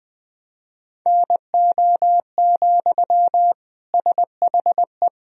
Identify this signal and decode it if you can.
Morse code, heard as a keyed tone, about 20 wpm, 700 Hz. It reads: NO, SHE
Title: Sentences from Top 1000 Words - Review Part 1 - 20wpm